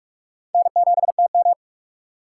Morse code, heard as a keyed tone, about 46 wpm, 700 Hz. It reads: N7TM
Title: USA Call Signs - 4 Characters - 3x Character Spacing - 45wpm